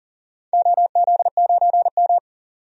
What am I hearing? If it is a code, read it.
Morse code, heard as a keyed tone, about 40 wpm, 700 Hz. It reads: OZ9M
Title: DXCC-100 Call Signs - 4 Characters - 40wpm